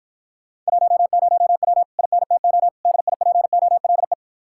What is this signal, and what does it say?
Morse code, heard as a keyed tone, about 58 wpm, 700 Hz. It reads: 10W INTO DIPOLE